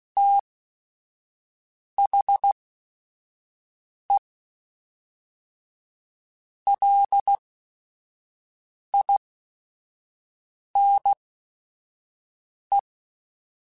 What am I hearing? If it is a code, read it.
Morse code, heard as a keyed tone, about 16 wpm, spaced as for 5 wpm, 800 Hz. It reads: THE LINE
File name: Technical Subjects / Technical Subjects-mp3s / Sw Pwr Sup @5